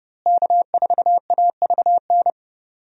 Morse code, heard as a keyed tone, about 31 wpm, 700 Hz. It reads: K4AVN
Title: USA Call Signs - 5 Characters - Repeated 3x - 30wpm